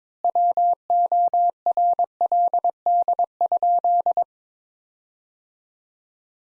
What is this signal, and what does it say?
Morse code, heard as a keyed tone, about 21 wpm, 700 Hz. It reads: WORLD?
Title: Sentences Top 200 Words - 22wpm